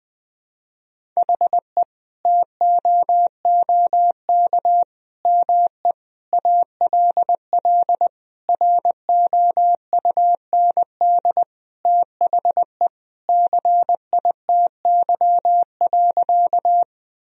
Morse code, heard as a keyed tone, about 19 wpm, 700 Hz. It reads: HE TOOK ME ALL ROUND THE CITY.
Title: Sentences from Top 1000 Words - Review Part 1 - 20wpm